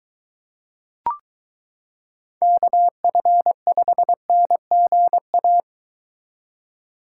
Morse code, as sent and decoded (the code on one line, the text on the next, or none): -.- ..-. ..... -. --. .-
KF5NGA